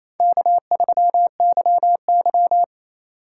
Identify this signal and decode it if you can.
Morse code, heard as a keyed tone, about 28 wpm, 700 Hz. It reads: K3YY
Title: USA Call Signs - 4 Characters - 4x Character Spacing - 28wpm